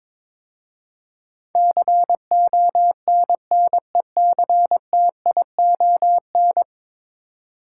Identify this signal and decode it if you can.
Morse code, heard as a keyed tone, about 22 wpm, 700 Hz. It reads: CONNECTION
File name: practice / 22wpm / 10 Letter Words - First 3 Letters Given - 22wpm